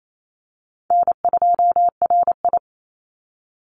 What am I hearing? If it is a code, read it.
Morse code, heard as a keyed tone, about 28 wpm, 700 Hz. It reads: N2RI
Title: Call Signs - 28wpm